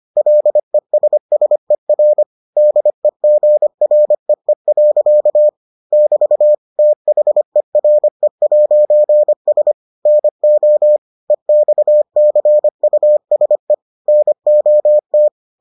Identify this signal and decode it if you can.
Morse code, heard as a keyed tone, about 25 wpm, 600 Hz. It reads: LESSER DEGREE. <BT> THERE'S NO EXCUSE NOT